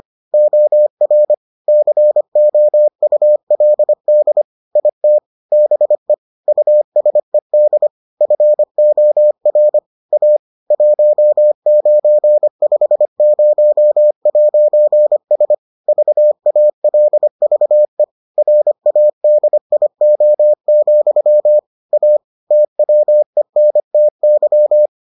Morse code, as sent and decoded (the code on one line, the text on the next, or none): --- .-. / -.-. --- ..- .-.. -.. / .. - / -... / ..- ... -.. / ..-. --- .-. / .- / .---- ----. ..... ----- .----. ... / ...- .- .-.. ...- . / .-. .- -.. .. --- --..-- / .- / - .-- . -. - -.--
OR COULD IT BE USED FOR A 1950'S VALVE RADIO, A TWENTY